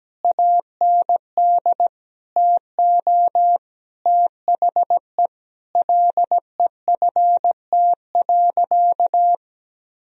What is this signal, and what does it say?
Morse code, heard as a keyed tone, about 17 wpm, 700 Hz. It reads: AND TO THE LEFT.
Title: Sentences from Top 1000 Words - Review Part 1 - 17wpm